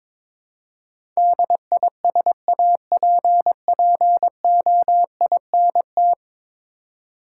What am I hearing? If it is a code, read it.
Morse code, heard as a keyed tone, about 22 wpm, 700 Hz. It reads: DISAPPOINT
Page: morse-code-ninja-practice-22wpm